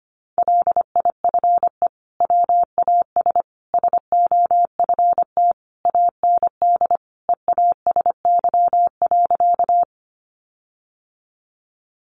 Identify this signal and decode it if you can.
Morse code, heard as a keyed tone, about 25 wpm, 700 Hz. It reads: LIFE WAS SOFT AND EASY.
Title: Sentences from Top 1500 Words 25wpm